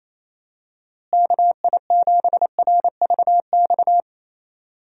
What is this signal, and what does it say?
Morse code, heard as a keyed tone, about 28 wpm, 700 Hz. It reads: KI7RVX
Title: USA Call Signs - 6 Characters - Repeated 2x - 28wpm